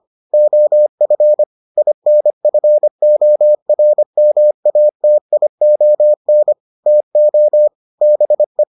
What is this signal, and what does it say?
Morse code, heard as a keyed tone, about 25 wpm, 600 Hz. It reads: OF INFORMATION TO BE